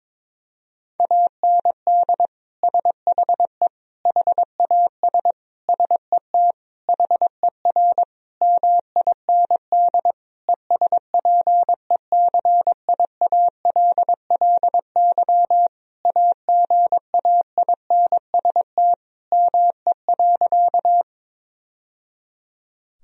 Morse code, heard as a keyed tone, about 22 wpm, 700 Hz. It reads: AND SHE HAS SET HER MIND ESPECIALLY AGAINST ME.
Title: Sentences from Top 1000 Words - Review Part 4 - 22wpm